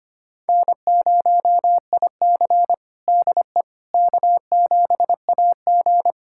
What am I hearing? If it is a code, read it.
Morse code, heard as a keyed tone, about 25 wpm, 700 Hz. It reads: N0IC DE K7AG
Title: USA Callsign DE Callsign - 4 Characters - 25wpm